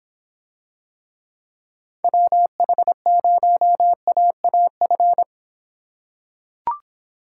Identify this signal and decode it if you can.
Morse code, heard as a keyed tone, about 23 wpm, 700 Hz. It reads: WH0AAF